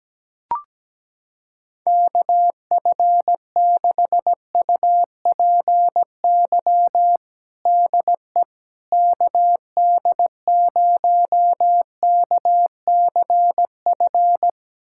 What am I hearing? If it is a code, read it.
Morse code, heard as a keyed tone, about 17 wpm, 700 Hz. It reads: KF6UPY DE KD0KCF
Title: USA Callsign DE Callsign - 6 Characters - 17wpm